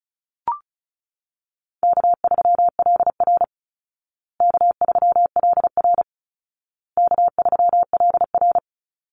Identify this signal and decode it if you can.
Morse code, heard as a keyed tone, about 34 wpm, 700 Hz. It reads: K3LR K3LR K3LR
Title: Call Signs Repeated 2x - 35wpm